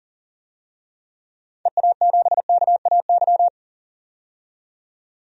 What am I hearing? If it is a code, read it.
Morse code, heard as a keyed tone, about 40 wpm, 700 Hz. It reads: EA7KAY
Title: DXCC-100 Call Signs - 6 Characters - 40wpm